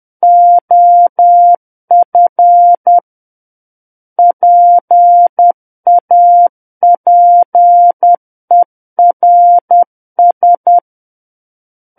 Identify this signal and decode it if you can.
Morse code, heard as a keyed tone, about 10 wpm, 700 Hz. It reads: OF PAPERS
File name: Book One / 01-Chapter 1 - 10 WPM